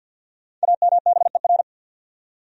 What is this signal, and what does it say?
Morse code, heard as a keyed tone, about 52 wpm, 700 Hz. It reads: AMBER
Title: USA Common Names - Max 5 Letters - 50wpm